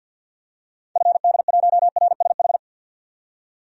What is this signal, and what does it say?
Morse code, heard as a keyed tone, about 48 wpm, 700 Hz. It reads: FD1RSH
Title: DXCC-100 Call Signs - 6 Characters - 50wpm